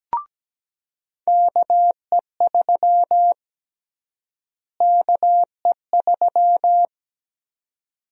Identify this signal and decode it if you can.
Morse code, heard as a keyed tone, about 17 wpm, 700 Hz. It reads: KE3 KE3
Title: USA Call Sign Prefixes - Repeated 2x - 17wpm